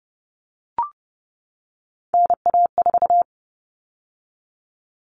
Morse code, heard as a keyed tone, about 30 wpm, 700 Hz. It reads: NA4
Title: All Cut Numbers - 3-Digit 30wpm